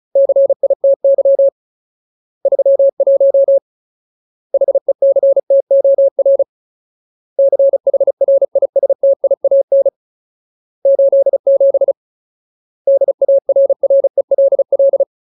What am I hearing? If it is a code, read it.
Morse code, heard as a keyed tone, about 34 wpm, 550 Hz. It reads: CITY 31 HECTOR CHRISTIAN 87 DARRELL
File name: Ultimate Word List / 35WPM Ultimate